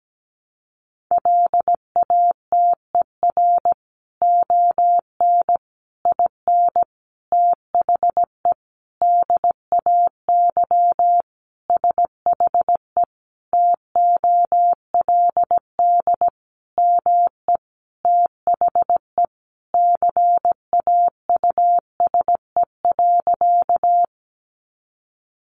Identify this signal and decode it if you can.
Morse code, heard as a keyed tone, about 17 wpm, 700 Hz. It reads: LATER ON IN THE DAY SHE TOLD ME THE CAUSE.
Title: Sentences from Top 700 Words 17wpm